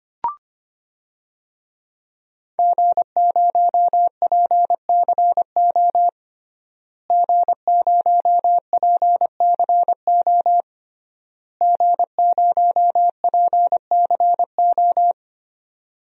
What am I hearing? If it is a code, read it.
Morse code, heard as a keyed tone, about 25 wpm, 700 Hz. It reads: G0PCO G0PCO G0PCO
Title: DXCC-100 Call Signs - 5 Characters - Repeated 3x - 25wpm